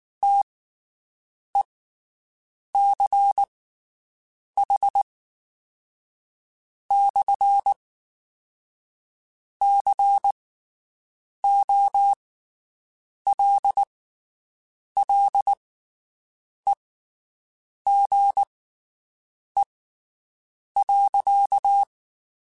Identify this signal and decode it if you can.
Morse code, heard as a keyed tone, about 19 wpm, spaced as for 6 wpm, 800 Hz. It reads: TECH/COLLEGE.
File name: (spaced) / QSOs Spaced-mp3s / WD0BGP 1 @18 sp